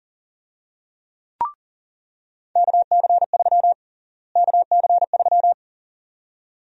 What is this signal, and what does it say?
Morse code, heard as a keyed tone, about 40 wpm, 700 Hz. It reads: KC3 KC3
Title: USA Call Sign Prefixes - Repeated 2x - 40wpm